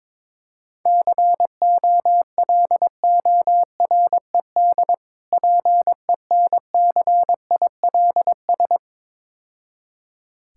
Morse code, heard as a keyed tone, about 22 wpm, 700 Hz. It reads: COLORED PENCILS
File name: Easy Words - Speed-Racing with Hints - 22wpm